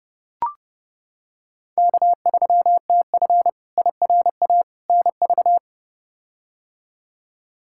Morse code, heard as a keyed tone, about 30 wpm, 700 Hz. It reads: K3TF IRA NV
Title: K1USN SST QSO Practice - 30wpm